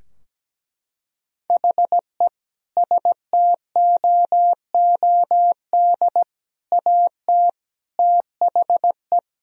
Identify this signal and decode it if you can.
Morse code, heard as a keyed tone, about 17 wpm, 700 Hz. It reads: HE STOOD AT THE
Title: Sentences from Top 1000 Words - Review Part 3 - 17wpm